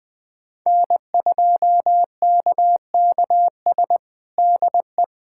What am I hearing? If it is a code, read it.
Morse code, heard as a keyed tone, about 20 wpm, 700 Hz. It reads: N2KKS DE
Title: USA Callsign DE Callsign - 5 Characters - 20wpm